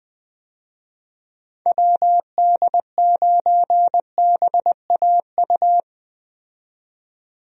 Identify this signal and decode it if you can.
Morse code, heard as a keyed tone, about 21 wpm, 700 Hz. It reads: WD9BAU
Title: USA Call Signs - 6 Characters - 20wpm